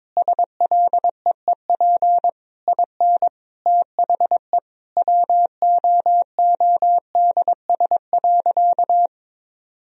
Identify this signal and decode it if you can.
Morse code, heard as a keyed tone, about 22 wpm, 700 Hz. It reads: SLEEP IN THE WOODS.
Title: Sentences from Top 1000 Words - Review Part 2 - 22wpm